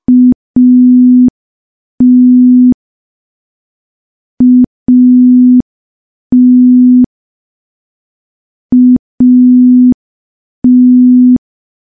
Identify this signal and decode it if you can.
Morse code, heard as a keyed tone, about 5 wpm, 250 Hz. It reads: AT AT AT
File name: morse / morse-training-05WPM